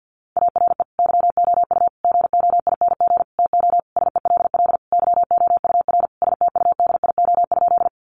Morse code, heard as a keyed tone, about 50 wpm, 700 Hz. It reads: ARE YOU GOING TO SELL YOUR STUDIO?
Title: Sentences from Top 1200 Words 50wpm